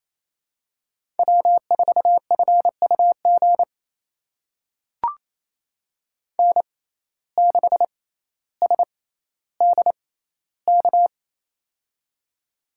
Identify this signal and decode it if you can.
Morse code, heard as a keyed tone, about 28 wpm, spaced as for 14 wpm, 700 Hz. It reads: W4FUG N6SDK
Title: USA Call Signs - 5 Characters - 6x Character Spacing - 28wpm